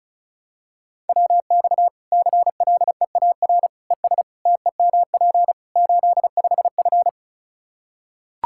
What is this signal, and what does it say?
Morse code, heard as a keyed tone, about 34 wpm, 700 Hz. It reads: WX CLEAR ES TEMP 85F
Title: Practice QSO - Exchange 2 - Info - 35wpm